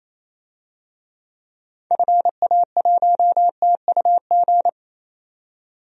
Morse code, heard as a keyed tone, about 28 wpm, 700 Hz. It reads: FA1TUG
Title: DXCC-100 Call Signs - 6 Characters - 28wpm